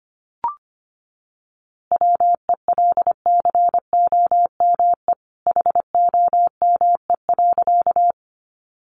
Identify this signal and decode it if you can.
Morse code, heard as a keyed tone, about 25 wpm, 700 Hz. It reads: WELCOME HOME.
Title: A Sound Of Thunder 25wpm